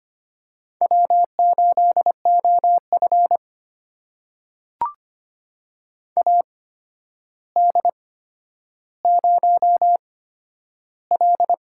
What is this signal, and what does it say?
Morse code, heard as a keyed tone, about 25 wpm, spaced as for 11 wpm, 700 Hz. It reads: W8OF AD0L
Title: USA Call Signs - 8x Character Spacing - 25wpm